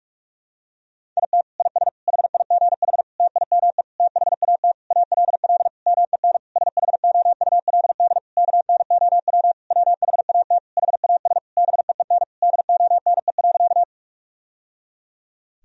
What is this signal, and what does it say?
Morse code, heard as a keyed tone, about 46 wpm, 700 Hz. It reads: IT IS HIGH TIME THAT ALL MEN SHOULD KNOW WHAT HAS BEEN DONE.
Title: Sentences from Top 1000 Words - Review Part 1 - 45wpm